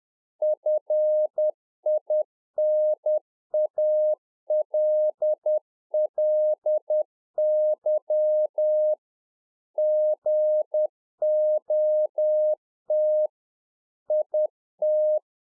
Morse code, heard as a keyed tone, about 10 wpm, 610 Hz. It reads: FINALLY GOT IT
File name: 2011-04-mp3 / A SMALL TRANCEIVER 1 10WPM